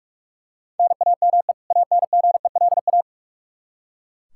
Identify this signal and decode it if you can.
Morse code, heard as a keyed tone, about 44 wpm, 700 Hz. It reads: NAME ANGELA
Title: K1USN SST QSO Practice 45wpm